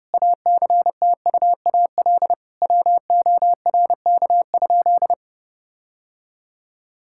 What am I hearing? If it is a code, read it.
Morse code, heard as a keyed tone, about 32 wpm, 700 Hz. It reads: ACTUAL WORK?
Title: Sentences from Top 1000 Words - Review Part 5 - 30wpm